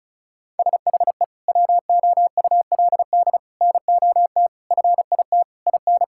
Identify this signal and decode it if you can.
Morse code, heard as a keyed tone, about 34 wpm, 700 Hz. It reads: SHE WOULD NOT FIT IN
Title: Sentences from Top 1600 Words 35wpm